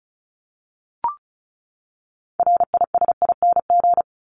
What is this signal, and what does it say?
Morse code, heard as a keyed tone, about 34 wpm, 700 Hz. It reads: RISING
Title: ING Suffix - Top 100 Words with Base Words 35wpm